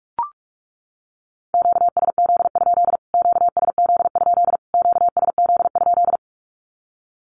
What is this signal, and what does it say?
Morse code, heard as a keyed tone, about 44 wpm, 700 Hz. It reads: QSZ? QSZ? QSZ?
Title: Q-Codes Repeated 3x 45wpm